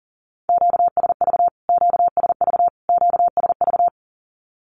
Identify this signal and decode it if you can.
Morse code, heard as a keyed tone, about 40 wpm, 700 Hz. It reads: QSV QSV QSV
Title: Q-Codes Repeated 3x 40wpm